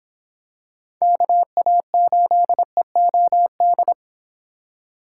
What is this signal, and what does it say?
Morse code, heard as a keyed tone, about 24 wpm, 700 Hz. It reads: KA8EOD